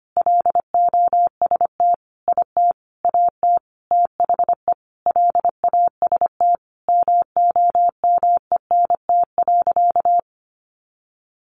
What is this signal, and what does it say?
Morse code, heard as a keyed tone, about 25 wpm, 700 Hz. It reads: LOST IT AT THE LAST MOMENT.
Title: Sentences from Top 1400 Words 25wpm